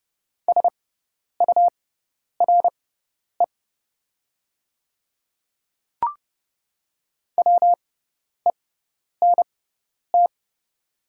Morse code, heard as a keyed tone, about 30 wpm, spaced as for 10 wpm, 700 Hz. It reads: SURE WENT
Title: 6x Character Spacing - Top 300 Words - 30wpm